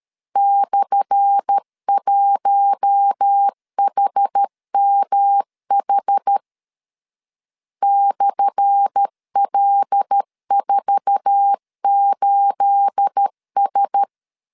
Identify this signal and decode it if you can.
Morse code, heard as a keyed tone, about 13 wpm, 800 Hz. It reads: /1HMH /L48S
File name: LEZIONI / 10 SIGNS / 10A SLASH 12wpm 2314